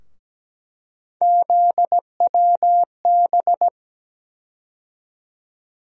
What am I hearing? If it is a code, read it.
Morse code, heard as a keyed tone, about 17 wpm, 700 Hz. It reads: ZWB
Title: USA Call Sign Postfixes - 3 Letters 17wpm